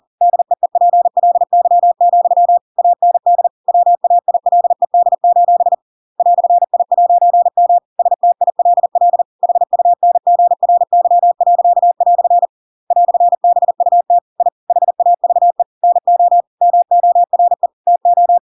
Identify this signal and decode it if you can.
Morse code, heard as a keyed tone, about 40 wpm, 700 Hz. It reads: DEEPLY, AND WAILED: "I'M STILL HUNGRY." "BUT I HAVE NO MORE TO
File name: Pinocchio / Pinocchio ch7 40 40 ik1hge